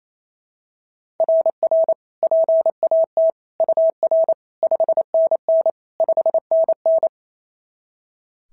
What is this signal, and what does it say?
Morse code, heard as a keyed tone, about 27 wpm, 690 Hz. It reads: RR PAT UR 5NN 5NN
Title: SOTA QSO Practice 28wpm